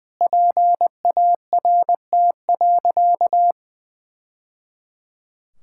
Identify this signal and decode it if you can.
Morse code, heard as a keyed tone, about 20 wpm, 700 Hz. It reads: PART.
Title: Sentences from Top 1000 Words - Review Part 5 - 20wpm